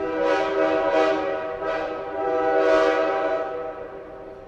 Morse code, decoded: none